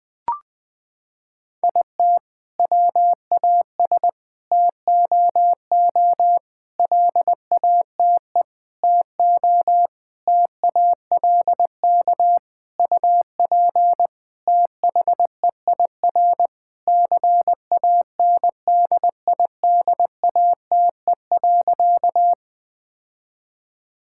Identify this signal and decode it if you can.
Morse code, heard as a keyed tone, about 20 wpm, 700 Hz. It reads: IT WAS TOO LATE TO TALK UP THEIR CANDIDATE.
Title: Sentences from Top 1600 Words 20wpm